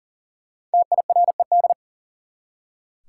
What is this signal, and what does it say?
Morse code, heard as a keyed tone, about 40 wpm, 700 Hz. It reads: TIRED
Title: Five Letter Words - 40wpm